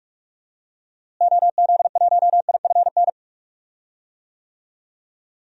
Morse code, heard as a keyed tone, about 46 wpm, 700 Hz. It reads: OZ1IFN
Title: DXCC-100 Call Signs - 6 Characters - Repeated 3x - 45wpm